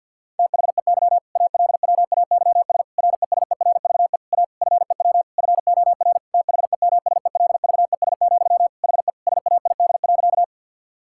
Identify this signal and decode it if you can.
Morse code, heard as a keyed tone, about 51 wpm, 700 Hz. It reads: THEY ALWAYS RESERVE A FEW FOR THEMSELVES, HE SAID.
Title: Sentences from Top 1000 Words - Review Part 6 - 50wpm